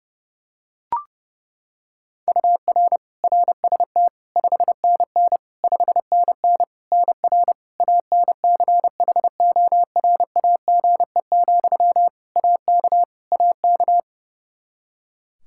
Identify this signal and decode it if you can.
Morse code, heard as a keyed tone, about 30 wpm, 700 Hz. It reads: UR RST 5NN 5NN NR ANCHORAGE, AK AK
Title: Practice QSO - Exchange 1 - Beginning - 30wpm